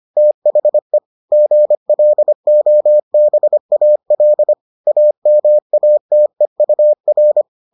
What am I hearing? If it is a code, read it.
Morse code, heard as a keyed tone, about 25 wpm, 600 Hz. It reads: THE GLOBAL AMATEUR